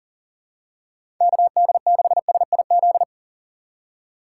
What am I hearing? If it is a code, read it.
Morse code, heard as a keyed tone, about 40 wpm, 700 Hz. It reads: KD6SIZ
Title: USA Call Signs - 6 Characters - 6x Character Spacing - 40wpm